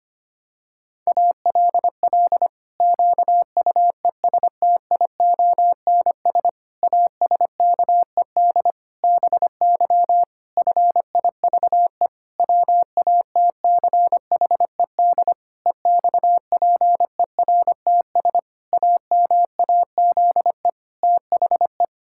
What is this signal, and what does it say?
Morse code, heard as a keyed tone, about 25 wpm, 700 Hz. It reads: ALL QUESTIONS ASKED BY FIVE WATCHED EXPERTS AMAZE THE
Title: Pangram Challenge v2 - 25wpm